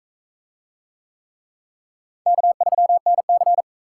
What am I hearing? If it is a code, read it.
Morse code, heard as a keyed tone, about 43 wpm, 700 Hz. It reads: K3NC